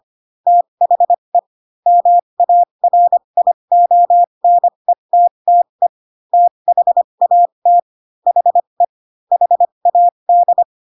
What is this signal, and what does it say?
Morse code, heard as a keyed tone, about 25 wpm, spaced as for 22 wpm, 700 Hz. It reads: THE MARIONETTE THAT HE HAD